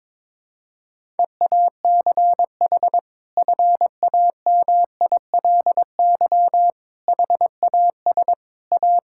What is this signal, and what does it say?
Morse code, heard as a keyed tone, about 21 wpm, 700 Hz. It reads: EACH FAMILY HAS A
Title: Sentences from Top 1000 Words - Review Part 5 - 22wpm